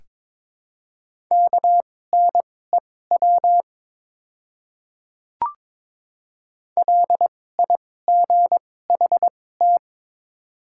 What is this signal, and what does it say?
Morse code, heard as a keyed tone, about 22 wpm, spaced as for 16 wpm, 690 Hz. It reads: KNEW LIGHT
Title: 2x Character Spacing - Top 500 Words - 22wpm